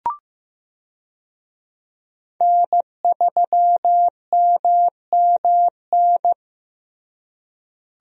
Morse code, heard as a keyed tone, about 15 wpm, 700 Hz. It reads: N3MMN
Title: USA Call Signs - 5 Characters - Repeated 3x - 15wpm